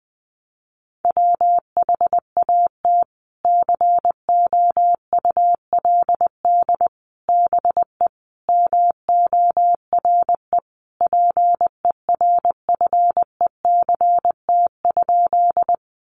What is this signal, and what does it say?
Morse code, heard as a keyed tone, about 19 wpm, 700 Hz. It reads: WHAT COULD BE MORE PERFECT?
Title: Sentences from Top 1000 Words 20wpm